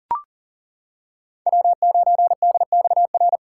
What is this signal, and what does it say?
Morse code, heard as a keyed tone, about 40 wpm, 700 Hz. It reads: W9DXR
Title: K1USN SST QSO Practice - 40wpm